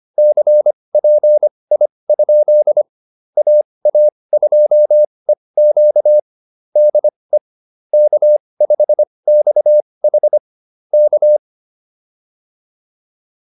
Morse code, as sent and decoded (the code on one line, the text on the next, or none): -.-. .--. .. ..--.. / .- .- ..--- . --.- / -.. / -.- ..... -..- .... / -.-
CPI? AA2EQ DE K5XH K